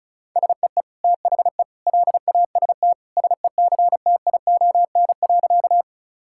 Morse code, heard as a keyed tone, about 34 wpm, 700 Hz. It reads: SEE THE LAST SECTION.
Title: Sentences from Top 1000 Words - Review Part 5 - 35wpm